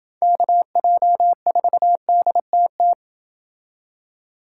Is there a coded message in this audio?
Yes: Morse code from 0.2 to 2.9 s, heard as a keyed tone, about 28 wpm, 700 Hz. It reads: KJ4DTT